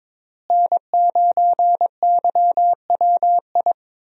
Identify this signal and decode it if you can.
Morse code, heard as a keyed tone, about 22 wpm, 700 Hz. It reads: N9YWI
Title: USA Call Signs - 5 Characters - Repeated 3x - 22wpm